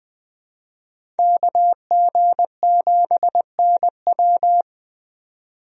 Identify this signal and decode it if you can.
Morse code, heard as a keyed tone, about 20 wpm, 700 Hz. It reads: KG7NW
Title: USA Call Signs - 5 Characters - 20wpm